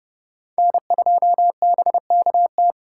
Morse code, heard as a keyed tone, about 32 wpm, 700 Hz. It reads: N2BKT
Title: USA Call Signs - 5 Characters - 4x Character Spacing - 30wpm